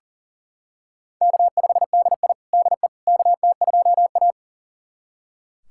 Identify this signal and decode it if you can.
Morse code, heard as a keyed tone, about 40 wpm, 700 Hz. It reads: K5DI DE KT2A
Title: USA Callsign DE Callsign - 4 Characters - 40wpm